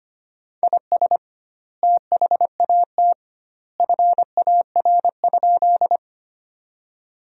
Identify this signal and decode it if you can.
Morse code, heard as a keyed tone, about 25 wpm, 700 Hz. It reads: IS THAT FAR?
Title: 2x Word-Spacing - Sentences from Top 300 Words - 25wpm